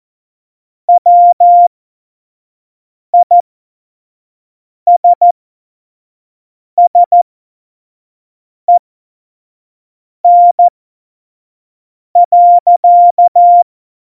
Morse code, heard as a keyed tone, about 14 wpm, spaced as for 5 wpm, 700 Hz. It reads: WISSEN.
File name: Morsekurs / Witze / Witz 30. Juli 2019, 25 F70